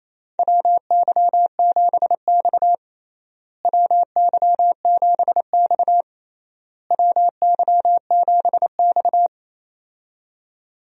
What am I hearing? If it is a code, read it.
Morse code, heard as a keyed tone, about 27 wpm, 700 Hz. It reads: WY7X WY7X WY7X
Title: USA Call Signs - 4 Characters - Repeated 3x - 28wpm